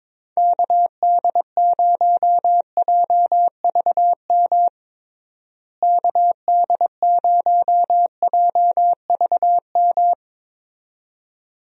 Morse code, heard as a keyed tone, about 22 wpm, 700 Hz. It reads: KD0JVM KD0JVM
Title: USA Call Signs - 6 Characters - Repeated 2x - 22wpm